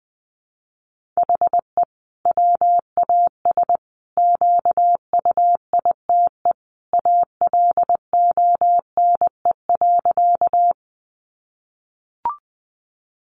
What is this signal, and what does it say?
Morse code, heard as a keyed tone, about 20 wpm, 700 Hz. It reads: HE WAS QUITE ALONE.
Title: Sentences from Top 1000 Words 20wpm